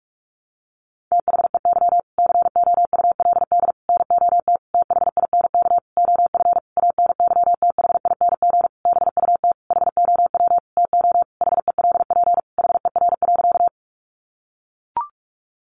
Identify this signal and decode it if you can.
Morse code, heard as a keyed tone, about 44 wpm, 700 Hz. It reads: THEY COULD NOT THINK OF ANYTHING BUT HOW TO HELP HER.
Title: Sentences from Top 1000 Words 45wpm